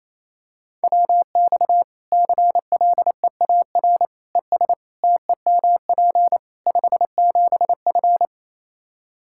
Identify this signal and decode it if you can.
Morse code, heard as a keyed tone, about 28 wpm, 700 Hz. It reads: WX CLEAR ES TEMP 57F
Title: Practice QSO - Exchange 2 - Info - 28wpm